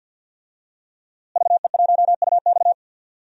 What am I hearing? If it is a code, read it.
Morse code, heard as a keyed tone, about 46 wpm, 700 Hz. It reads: VE1UX
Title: DXCC-100 Call Signs - 5 Characters - Repeated 3x - 50wpm